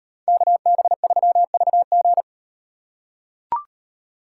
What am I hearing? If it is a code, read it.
Morse code, heard as a keyed tone, about 38 wpm, 700 Hz. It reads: KB3VG